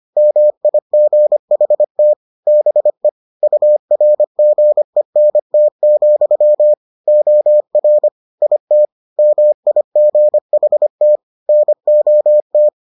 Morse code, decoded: MIGHT BE URGENT, OR IT MIGHT NOT

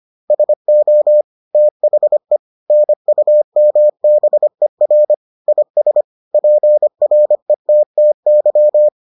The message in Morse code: SO THE NUMBER IS PRETTY

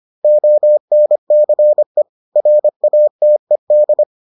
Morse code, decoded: ONCE RATED

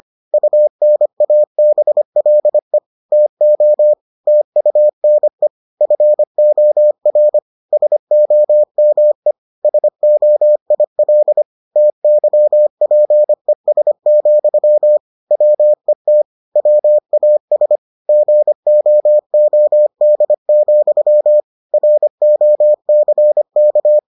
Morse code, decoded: UNABLE TO TUNE FOR SOME SOIL TYPES, WET WAS GOOD, ROCK

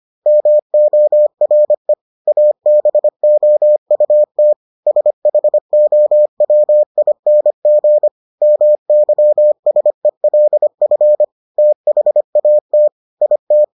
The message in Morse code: MORE ABOUT SHOWING MYSELF THAT IT

